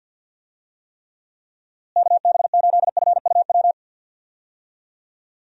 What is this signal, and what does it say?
Morse code, heard as a keyed tone, about 51 wpm, 700 Hz. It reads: KB8FUW
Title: USA Call Signs - 6 Characters - Repeated 3x - 50wpm